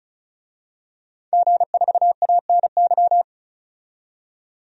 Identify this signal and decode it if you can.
Morse code, heard as a keyed tone, about 34 wpm, 700 Hz. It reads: G4ANY
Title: DXCC-100 Call Signs - 5 Characters - 35wpm